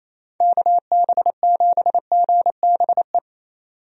Morse code, heard as a keyed tone, about 28 wpm, 700 Hz. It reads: KB7GBE
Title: USA Call Signs - 6 Characters - 28wpm